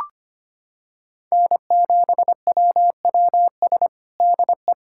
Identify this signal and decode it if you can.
Morse code, heard as a keyed tone, about 25 wpm, 700 Hz. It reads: N7WWS DE
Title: USA Callsign DE Callsign - 5 Characters - 25wpm